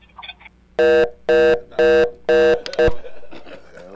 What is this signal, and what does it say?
Reorder tone (North America), 480 + 620 Hz, with speech.